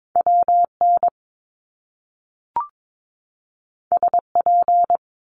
Morse code, heard as a keyed tone, about 22 wpm, 700 Hz. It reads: WN SP
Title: Top Two Letter Combinations in Words 22wpm